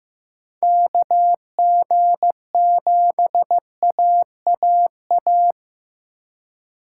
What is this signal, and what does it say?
Morse code, heard as a keyed tone, about 15 wpm, 700 Hz. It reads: KG7AAA
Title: USA Call Signs - 3x Character Spacing - 15wpm